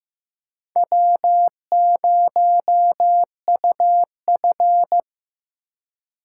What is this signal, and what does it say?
Morse code, heard as a keyed tone, about 15 wpm, 700 Hz. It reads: W0UF
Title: K1USN SST QSO Practice - 15wpm